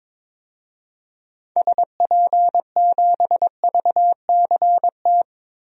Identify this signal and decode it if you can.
Morse code, heard as a keyed tone, about 22 wpm, 700 Hz. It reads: SP7VCT
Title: DXCC-100 Call Signs - 6 Characters - 22wpm